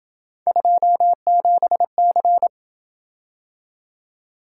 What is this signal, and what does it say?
Morse code, heard as a keyed tone, about 27 wpm, 700 Hz. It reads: 27C